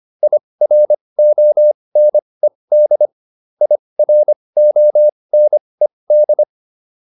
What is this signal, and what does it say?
Morse code, heard as a keyed tone, about 25 wpm, spaced as for 20 wpm, 600 Hz. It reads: IRONED IRONED